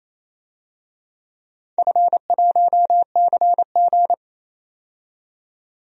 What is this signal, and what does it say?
Morse code, heard as a keyed tone, about 28 wpm, 700 Hz. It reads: F1CG
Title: DXCC-100 Call Signs - 4 Characters - 28wpm